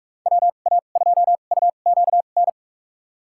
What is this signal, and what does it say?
Morse code, heard as a keyed tone, about 46 wpm, spaced as for 33 wpm, 700 Hz. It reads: WA2UQN
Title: USA Call Signs - 6 Characters - 2x Character Spacing - 45wpm